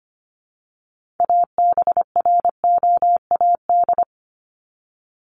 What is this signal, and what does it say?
Morse code, heard as a keyed tone, about 25 wpm, 700 Hz. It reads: ABROAD
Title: Top 1751-2000 Words 25wpm